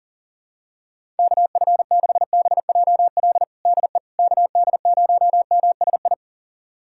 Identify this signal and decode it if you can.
Morse code, heard as a keyed tone, about 40 wpm, 700 Hz. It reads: KF6BJL DE KD0MSI
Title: USA Callsign DE Callsign - 6 Characters - 40wpm